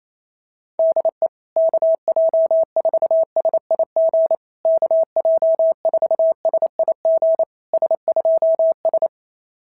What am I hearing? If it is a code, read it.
Morse code, heard as a keyed tone, about 27 wpm, 650 Hz. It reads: DE KJ4SIG KJ4SIG S2S
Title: SOTA QSO Practice 28wpm